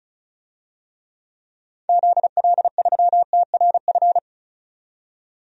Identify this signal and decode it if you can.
Morse code, heard as a keyed tone, about 34 wpm, 700 Hz. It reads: ZL3TRF
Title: DXCC-100 Call Signs - 6 Characters - 35wpm